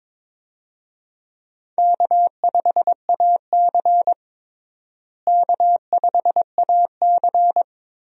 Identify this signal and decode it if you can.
Morse code, heard as a keyed tone, about 21 wpm, 700 Hz. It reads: K5AC K5AC
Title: USA Call Signs - 4 Characters - Repeated 2x - 22wpm